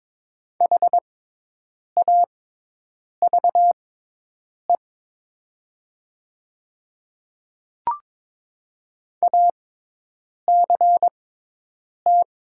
Morse code, heard as a keyed tone, about 22 wpm, spaced as for 8 wpm, 690 Hz. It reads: HAVE ACT
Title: 6x Character Spacing - Top 300 Words - 22wpm